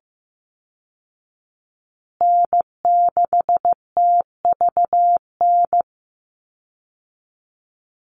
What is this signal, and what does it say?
Morse code, heard as a keyed tone, about 15 wpm, 700 Hz. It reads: N6TVN
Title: Call Signs Repeated 2x - 15wpm